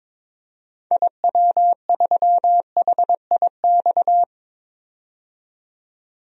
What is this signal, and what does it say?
Morse code, heard as a keyed tone, about 22 wpm, 700 Hz. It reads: IW3HIX
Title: DXCC-100 Call Signs - 6 Characters - 22wpm